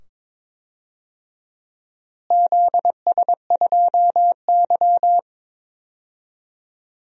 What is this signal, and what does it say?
Morse code, heard as a keyed tone, about 22 wpm, 700 Hz. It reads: ZS2Y
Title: DXCC-100 Call Signs - 4 Characters - Repeated 2x - 22wpm